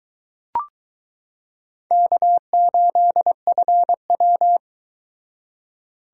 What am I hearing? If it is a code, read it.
Morse code, heard as a keyed tone, about 20 wpm, 700 Hz. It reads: K8FW